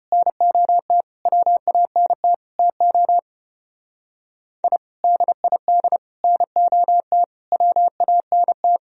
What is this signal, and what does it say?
Morse code, heard as a keyed tone, about 33 wpm, 700 Hz. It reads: NOT WANT TO I DID NOT WANT